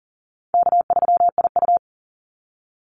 Morse code, heard as a keyed tone, about 40 wpm, 700 Hz. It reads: K3IU
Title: Easy Call Signs - 40wpm